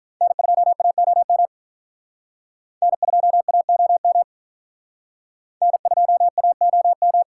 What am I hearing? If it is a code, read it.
Morse code, heard as a keyed tone, about 46 wpm, 700 Hz. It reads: N2AOM N2AOM N2AOM